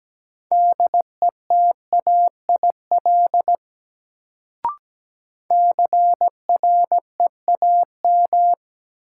Morse code, heard as a keyed tone, about 17 wpm, 700 Hz. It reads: DETAIL CREAM
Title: Sentences from Top 1500 Words - New Word Drill Alt - 17wpm